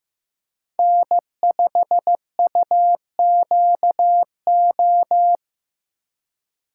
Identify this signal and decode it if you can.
Morse code, heard as a keyed tone, about 15 wpm, 700 Hz. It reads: N5UQO
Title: USA Call Signs - 5 Characters - 3x Character Spacing - 15wpm